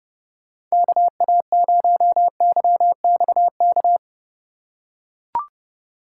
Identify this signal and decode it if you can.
Morse code, heard as a keyed tone, about 30 wpm, 700 Hz. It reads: KA0YXK
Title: USA Call Signs - 6 Characters - 2x Character Spacing - 30wpm